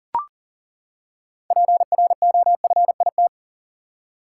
Morse code, heard as a keyed tone, about 37 wpm, 700 Hz. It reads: PROFIT
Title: Six Letter Words - 40wpm